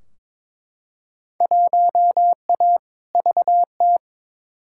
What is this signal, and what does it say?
Morse code, heard as a keyed tone, about 22 wpm, 700 Hz. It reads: 1A VT
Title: Field Day Exchanges Encore - 22wpm